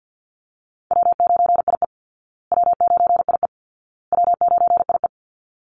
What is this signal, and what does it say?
Morse code, heard as a keyed tone, about 48 wpm, 700 Hz. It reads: W9IE W9IE W9IE
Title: Easy Call Signs Repeated 3x - 50wpm